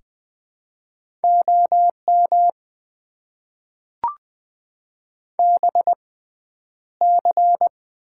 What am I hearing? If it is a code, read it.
Morse code, heard as a keyed tone, about 20 wpm, spaced as for 10 wpm, 700 Hz. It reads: OM BC
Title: Top 90 QSO Elements - 6x Character Spacing - 20wpm